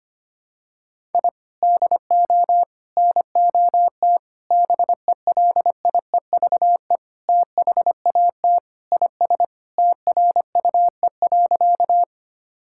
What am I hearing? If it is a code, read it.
Morse code, heard as a keyed tone, about 25 wpm, 700 Hz. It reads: I DO NOT BELIEVE THAT IS TRUE.
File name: practice / 25wpm / Sentences with 4-Word Phrases - 25wpm